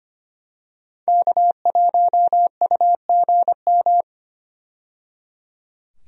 Morse code, heard as a keyed tone, about 25 wpm, 700 Hz. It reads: K1UGM
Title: USA Call Signs - 5 Characters - 25wpm